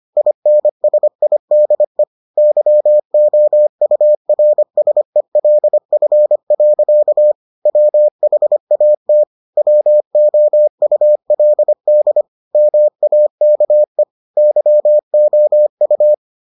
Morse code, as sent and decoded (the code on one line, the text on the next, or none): .. -. ... .. -.. / -.-- --- ..- .-. ... .-.. ..-. .-.-.- / .-- .... .- - / .-- --- ..- .-.. -.. / -- .- -.- . / -.-- --- ..-
INSIDE YOURSELF. WHAT WOULD MAKE YOU